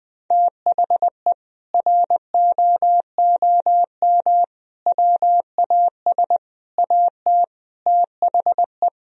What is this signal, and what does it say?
Morse code, heard as a keyed tone, about 20 wpm, 700 Hz. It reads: THE ROOM WAS AT THE
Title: Sentences from Top 1000 Words - Review Part 2 - 20wpm